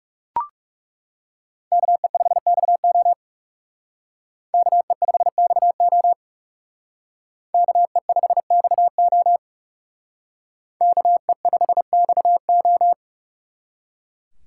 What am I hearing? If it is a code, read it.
Morse code, heard as a keyed tone, about 39 wpm, 700 Hz. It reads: KE5XO KE5XO KE5XO KE5XO